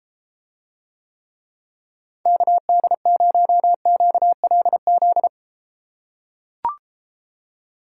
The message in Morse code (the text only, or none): KD0QLZ